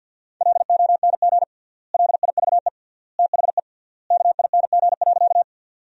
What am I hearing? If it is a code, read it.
Morse code, heard as a keyed tone, about 50 wpm, 700 Hz. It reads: LONG LIVE THE KING.
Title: 3x Word-Spacing - Sentences from Top 400 Words - 50wpm